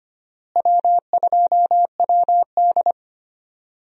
Morse code, heard as a keyed tone, about 25 wpm, 700 Hz. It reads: W2WD
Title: USA Call Signs - 4 Characters - Repeated 2x - 25wpm